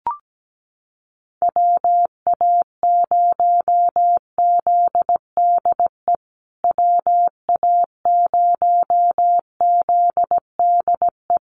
Morse code, heard as a keyed tone, about 17 wpm, 700 Hz. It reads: WA0ZDE WA0ZDE
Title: USA Call Signs Repeated 3x 17wpm